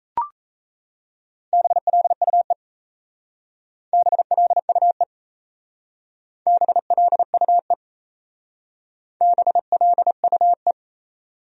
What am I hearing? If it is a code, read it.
Morse code, heard as a keyed tone, about 37 wpm, 700 Hz. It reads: BLUE BLUE BLUE BLUE